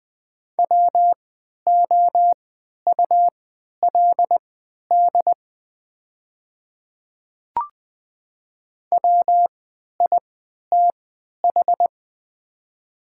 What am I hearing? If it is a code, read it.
Morse code, heard as a keyed tone, about 20 wpm, spaced as for 11 wpm, 700 Hz. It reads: WOULD WITH
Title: 3x Character Spacing - Top 200 Words - 20wpm